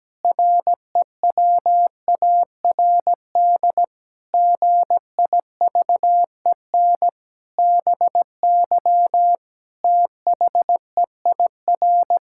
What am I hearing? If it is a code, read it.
Morse code, heard as a keyed tone, about 17 wpm, 700 Hz. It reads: REWARD GIVEN BY THEIR
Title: Pangram Challenge v2 - 17wpm